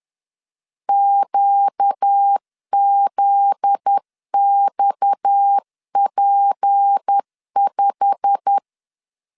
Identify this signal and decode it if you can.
Morse code, heard as a keyed tone, about 10 wpm, 800 Hz. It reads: QZXP5
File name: LEZIONI / 9 28 10wpm 2842